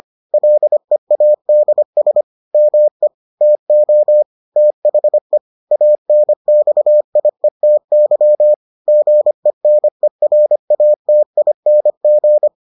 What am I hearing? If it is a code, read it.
Morse code, heard as a keyed tone, about 25 wpm, 600 Hz. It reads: LEADS ME TO THE ANXIETY GENERATING